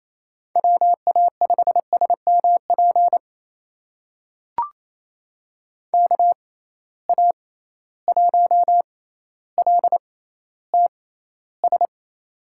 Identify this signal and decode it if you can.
Morse code, heard as a keyed tone, about 28 wpm, spaced as for 14 wpm, 700 Hz. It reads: WA5SMP KA1LTS
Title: USA Call Signs - 6 Characters - 6x Character Spacing - 28wpm